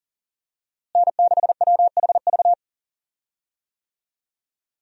Morse code, heard as a keyed tone, about 40 wpm, 700 Hz. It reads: N6WHV
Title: USA Call Signs - 5 Characters - Repeated 2x - 40wpm